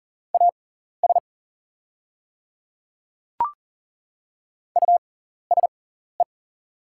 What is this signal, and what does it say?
Morse code, heard as a keyed tone, about 40 wpm, spaced as for 14 wpm, 690 Hz. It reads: AS USE
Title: 6x Character Spacing - Top 200 Words - 40wpm